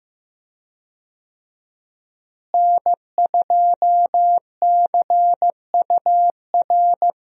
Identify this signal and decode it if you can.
Morse code, heard as a keyed tone, about 15 wpm, 700 Hz. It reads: N2CUR
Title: USA Call Signs - 5 Characters - Repeated 3x - 15wpm